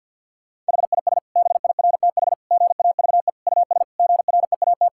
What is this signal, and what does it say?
Morse code, heard as a keyed tone, about 52 wpm, 700 Hz. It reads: HIS BIRTH GAVE US GREAT
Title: Sentences from Top 1000 Words - Review Part 5 - 50wpm